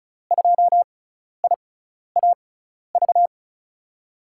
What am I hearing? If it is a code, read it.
Morse code, heard as a keyed tone, about 36 wpm, spaced as for 12 wpm, 700 Hz. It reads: 2IAV
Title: USA Call Signs - 6x Character Spacing - 35wpm